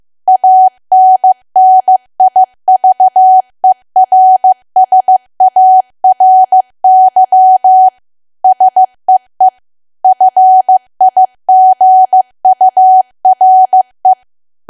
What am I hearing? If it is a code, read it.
Morse code, heard as a keyed tone, about 15 wpm, 750 Hz. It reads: ANNIVERSARY SEE FIGURE
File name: cwa / W1AW-Level2 / 15WPM